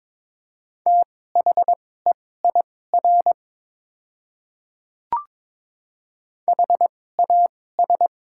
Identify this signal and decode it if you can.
Morse code, heard as a keyed tone, about 22 wpm, spaced as for 16 wpm, 700 Hz. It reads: THEIR HAS
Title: 2x Character Spacing - Top 200 Words - 22wpm